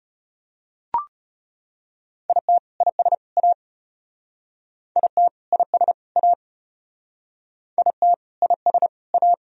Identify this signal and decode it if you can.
Morse code, heard as a keyed tone, about 33 wpm, 700 Hz. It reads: IT IS A IT IS A IT IS A